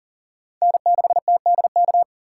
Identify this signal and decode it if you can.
Morse code, heard as a keyed tone, about 40 wpm, 700 Hz. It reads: N6TDK
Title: USA Call Signs - 5 Characters - 2x Character Spacing - 40wpm